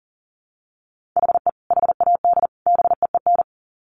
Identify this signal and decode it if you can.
Morse code, heard as a keyed tone, about 37 wpm, 700 Hz. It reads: HE HAD BEEN
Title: Sets of 3 Words 40wpm